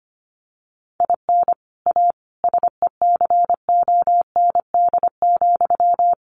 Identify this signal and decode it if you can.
Morse code, heard as a keyed tone, about 25 wpm, 700 Hz. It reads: IN A SECOND,
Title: Sentences from Top 1500 Words 25wpm